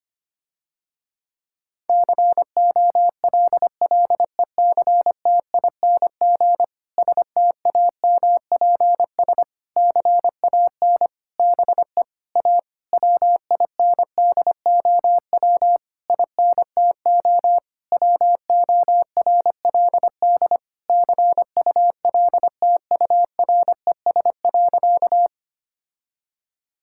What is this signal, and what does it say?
Morse code, heard as a keyed tone, about 25 wpm, 700 Hz. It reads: COLLECTING STAMPS CAN BE A WINDOW INTO WORLD CULTURES.
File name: practice / 25wpm / Difficult Sentences with Hints - 25wpm